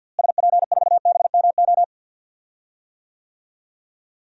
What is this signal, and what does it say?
Morse code, heard as a keyed tone, about 52 wpm, 700 Hz. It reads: SP4BMO
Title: DXCC-100 Call Signs - 6 Characters - Repeated 3x - 50wpm